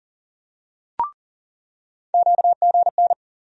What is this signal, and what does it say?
Morse code, heard as a keyed tone, about 40 wpm, 700 Hz. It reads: QGN
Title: USA Call Sign Postfixes - 3 Letters 40wpm